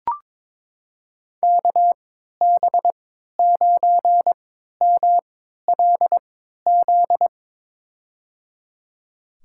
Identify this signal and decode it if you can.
Morse code, heard as a keyed tone, about 22 wpm, spaced as for 13 wpm, 700 Hz. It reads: KB9MLZ
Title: USA Call Signs - 6 Characters - 3x Character Spacing - 22wpm